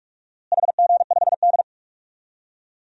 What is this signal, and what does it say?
Morse code, heard as a keyed tone, about 46 wpm, 700 Hz. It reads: HG5D
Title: DXCC-100 Call Signs - 4 Characters - Repeated 3x - 45wpm